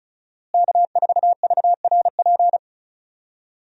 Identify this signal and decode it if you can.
Morse code, heard as a keyed tone, about 34 wpm, 700 Hz. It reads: K4VRP
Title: USA Call Signs - 5 Characters - Repeated 2x - 35wpm